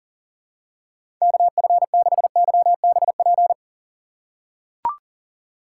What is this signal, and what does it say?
Morse code, heard as a keyed tone, about 40 wpm, 700 Hz. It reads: KF6YBP
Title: USA Call Signs - 3x Character Spacing - 40wpm